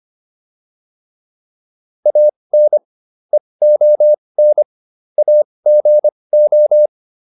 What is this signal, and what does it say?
Morse code, heard as a keyed tone, about 25 wpm, spaced as for 20 wpm, 600 Hz. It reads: AN EON AGO